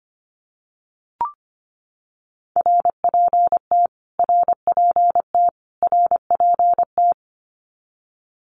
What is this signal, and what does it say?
Morse code, heard as a keyed tone, about 25 wpm, 700 Hz. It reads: RPT RPT RPT
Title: Abbreviations Repeated 3x - 25wpm